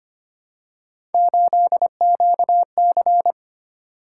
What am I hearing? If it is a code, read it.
Morse code, heard as a keyed tone, about 25 wpm, 700 Hz. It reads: 8QC
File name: Three Letter-Number Combinations - 25wpm